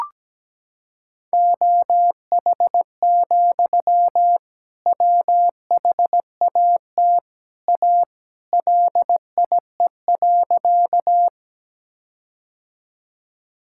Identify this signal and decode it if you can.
Morse code, heard as a keyed tone, about 17 wpm, 700 Hz. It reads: OH, WHAT A LIE.
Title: Sentences from Top 1600 Words 17wpm